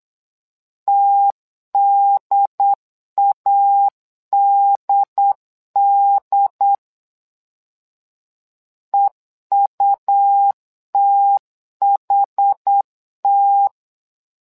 Morse code, heard as a keyed tone, about 8 wpm, 800 Hz. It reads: TDADD EUTHT
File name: LEZIONI / 2 UHDN / 2 UHD 8wpm 2258